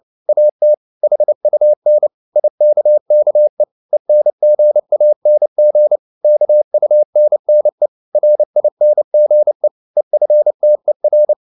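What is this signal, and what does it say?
Morse code, heard as a keyed tone, about 29 wpm, 600 Hz. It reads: AT HUN IKKE ENGANG KUNNE RINGE EFTER
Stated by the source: Fast (QRQ) Morse at 29 wpm, from CW med Gnister.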